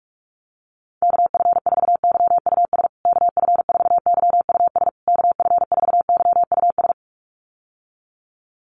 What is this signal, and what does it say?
Morse code, heard as a keyed tone, about 44 wpm, 690 Hz. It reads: KF4YUS KF4YUS KF4YUS
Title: USA Call Signs Repeated 3x 45wpm